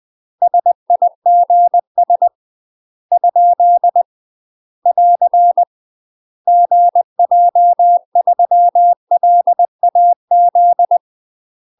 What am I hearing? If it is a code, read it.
Morse code, heard as a keyed tone, about 20 wpm, 700 Hz. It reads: SIGS ? <AR> GJ3LAZ
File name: UK MASTER QSOs TRAINING / 20WPM / UK MASTER QSOs 0028